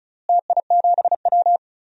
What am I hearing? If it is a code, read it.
Morse code, heard as a keyed tone, about 36 wpm, 700 Hz. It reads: TI7W
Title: DXCC-100 Call Signs - 4 Characters - Repeated 3x - 35wpm